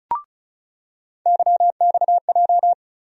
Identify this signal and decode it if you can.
Morse code, heard as a keyed tone, about 34 wpm, 700 Hz. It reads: YXJ
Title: USA Call Sign Postfixes - 3 Letters 35wpm